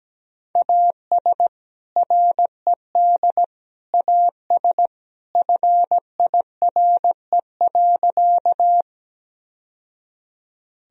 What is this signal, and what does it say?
Morse code, heard as a keyed tone, about 17 wpm, 700 Hz. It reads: AS RED AS FIRE.
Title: Sentences from Top 1000 Words - Review Part 2 - 17wpm